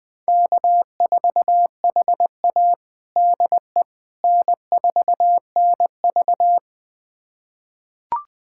Morse code, heard as a keyed tone, about 20 wpm, 700 Hz. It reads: K4HA DE N4NV
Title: USA Callsign DE Callsign - 4 Characters - 20wpm